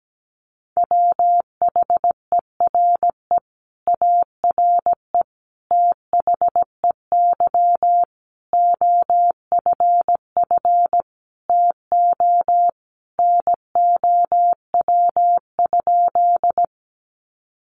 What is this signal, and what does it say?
Morse code, heard as a keyed tone, about 17 wpm, 700 Hz. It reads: WHERE ARE THEY OFF TO NOW?
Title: Sentences from Top 200 Words 17wpm